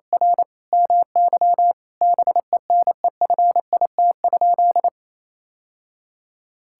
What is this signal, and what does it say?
Morse code, heard as a keyed tone, about 28 wpm, 700 Hz. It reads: R MY BENEFIT?
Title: Sentences from Top 1000 Words - Review Part 6 - 28wpm